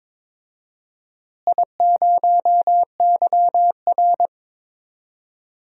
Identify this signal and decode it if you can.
Morse code, heard as a keyed tone, about 21 wpm, 700 Hz. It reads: I0YR
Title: DXCC-100 Call Signs - 4 Characters - Repeated 3x - 22wpm